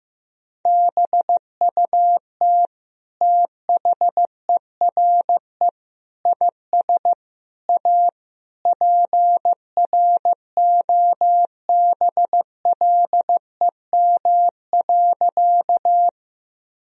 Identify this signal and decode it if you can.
Morse code, heard as a keyed tone, about 15 wpm, 700 Hz. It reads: BUT THERE IS A PROBLEM.
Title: Sentences from Top 1000 Words - Review Part 2 - 15wpm